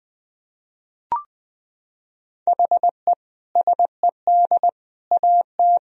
Morse code, heard as a keyed tone, about 20 wpm, 700 Hz. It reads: HE SED AT
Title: Sentences from Top 300 Words and Abbreviations 20wpm